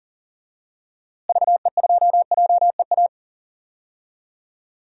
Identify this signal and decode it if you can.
Morse code, heard as a keyed tone, about 37 wpm, 700 Hz. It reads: VE2JEA
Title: DXCC-100 Call Signs - 6 Characters - Repeated 2x - 40wpm